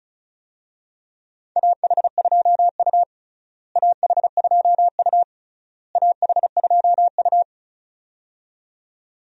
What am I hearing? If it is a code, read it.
Morse code, heard as a keyed tone, about 36 wpm, 700 Hz. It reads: AH2U AH2U AH2U
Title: USA Call Signs - 4 Characters - Repeated 3x - 35wpm